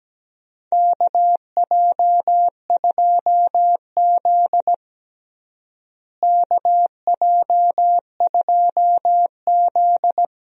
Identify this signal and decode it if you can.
Morse code, heard as a keyed tone, about 17 wpm, 700 Hz. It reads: KJ2Z KJ2Z
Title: USA Call Signs - 4 Characters - Repeated 2x - 17wpm